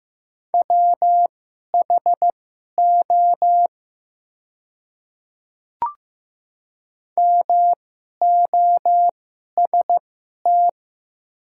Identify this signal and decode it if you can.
Morse code, heard as a keyed tone, about 15 wpm, spaced as for 11 wpm, 700 Hz. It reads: WHO MOST
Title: 2x Character Spacing - Top 100 Words - 15wpm